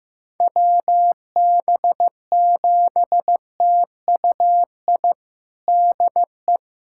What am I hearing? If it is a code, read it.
Morse code, heard as a keyed tone, about 15 wpm, 700 Hz. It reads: WB7TUI DE
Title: USA Callsign DE Callsign 15wpm